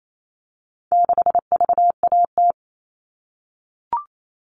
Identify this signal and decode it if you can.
Morse code, heard as a keyed tone, about 28 wpm, 700 Hz. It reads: 6VAT
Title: All Cut Numbers - 4-Digit 28wpm